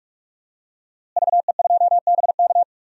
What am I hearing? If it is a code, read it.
Morse code, heard as a keyed tone, about 42 wpm, 700 Hz. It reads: VE2BK
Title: DXCC-100 Call Signs - 5 Characters - Repeated 3x - 45wpm